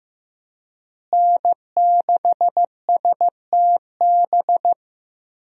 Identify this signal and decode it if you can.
Morse code, heard as a keyed tone, about 15 wpm, 700 Hz. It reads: N6STB
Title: USA Call Signs - 5 Characters - Repeated 2x - 15wpm